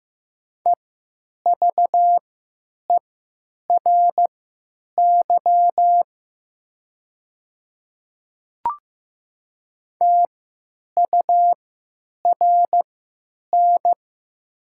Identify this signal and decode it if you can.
Morse code, heard as a keyed tone, about 15 wpm, spaced as for 9 wpm, 700 Hz. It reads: EVERY TURN
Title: 3x Character Spacing - Top 500 Words - 15wpm